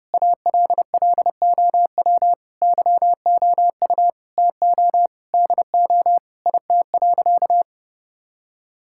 Morse code, heard as a keyed tone, about 30 wpm, 700 Hz. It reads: ALLOW YOU TO DO IT.
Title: Sentences from Top 1000 Words - Review Part 5 - 30wpm